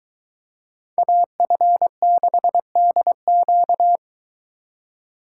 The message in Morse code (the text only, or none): AF6DQ